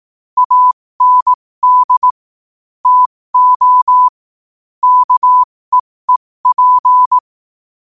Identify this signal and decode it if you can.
Morse code, heard as a keyed tone, about 18 wpm, spaced as for 15 wpm, 1000 Hz. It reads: AND TO KEEP
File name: qotdfiles / 15wpm